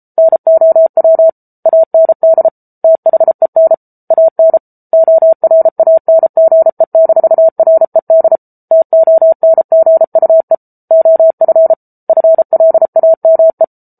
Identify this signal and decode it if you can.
Morse code, heard as a keyed tone, about 34 wpm, 650 Hz. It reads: NOW AND THEN AN ORANGE-RED TONGUE OF FLAME